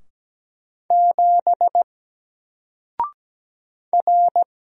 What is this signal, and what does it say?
Morse code, heard as a keyed tone, about 17 wpm, 700 Hz. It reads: 7 R
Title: Single Character (Letter:Number:Punctuation:Prosign) 17wpm